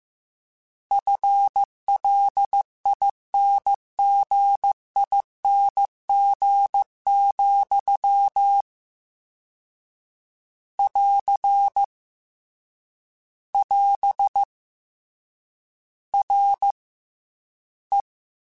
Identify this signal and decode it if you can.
Morse code, heard as a keyed tone, about 15 wpm, spaced as for 13 wpm, 800 Hz. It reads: FLINGING, <AR> <AS> RE
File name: Combinations / Combinations-mp3s / Combinations@13